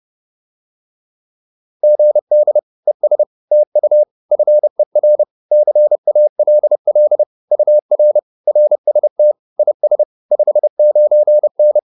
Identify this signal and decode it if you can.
Morse code, heard as a keyed tone, about 30 wpm, 600 Hz. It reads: GD ES TU FER CALL UR RST IS 59N